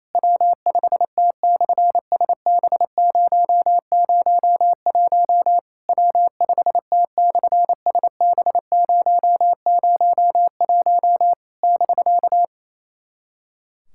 Morse code, heard as a keyed tone, about 28 wpm, 700 Hz. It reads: W5T/SB001 W5T/SB001 <BK>
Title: SOTA QSO Practice 28wpm